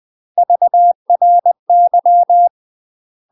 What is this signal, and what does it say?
Morse code, heard as a keyed tone, about 20 wpm, 700 Hz. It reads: VRY
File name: UK MASTER QSOs TRAINING/20WPM